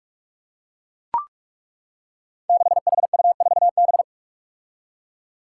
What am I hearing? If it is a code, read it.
Morse code, heard as a keyed tone, about 42 wpm, 700 Hz. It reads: 6HU4B
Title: Five Challenging Letter-Number Combinations 45wpm